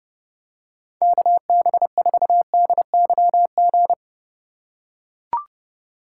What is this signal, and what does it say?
Morse code, heard as a keyed tone, about 30 wpm, 700 Hz. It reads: KB4DYG